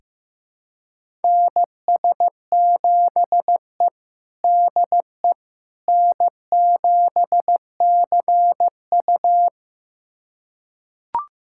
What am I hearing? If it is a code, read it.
Morse code, heard as a keyed tone, about 15 wpm, 700 Hz. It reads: NS7E DE N7CU
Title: USA Callsign DE Callsign - 4 Characters - 15wpm